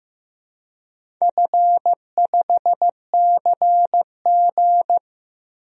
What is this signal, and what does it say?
Morse code, heard as a keyed tone, about 15 wpm, 700 Hz. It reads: F5CG
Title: DXCC-100 Call Signs - 4 Characters - Repeated 3x - 15wpm